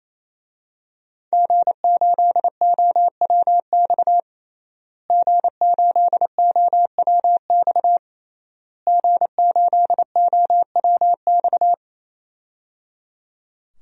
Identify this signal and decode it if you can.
Morse code, heard as a keyed tone, about 28 wpm, 700 Hz. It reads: G8OWX G8OWX G8OWX
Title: DXCC-100 Call Signs - 5 Characters - Repeated 3x - 28wpm